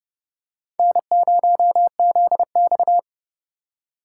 Morse code, heard as a keyed tone, about 31 wpm, 700 Hz. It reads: N0ZX
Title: USA Call Signs - 4 Characters - 4x Character Spacing - 30wpm